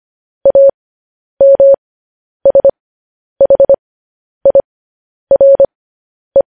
Morse code, heard as a keyed tone, about 25 wpm, spaced as for 10 wpm, 550 Hz. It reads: AMSHIRE
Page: news headlines in morse code at 10 wpm